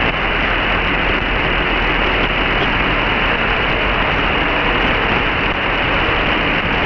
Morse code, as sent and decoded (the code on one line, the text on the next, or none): none